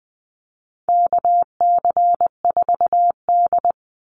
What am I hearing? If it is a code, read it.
Morse code, heard as a keyed tone, about 20 wpm, 700 Hz. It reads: KC4D
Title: Call Signs Repeated 3x - 20wpm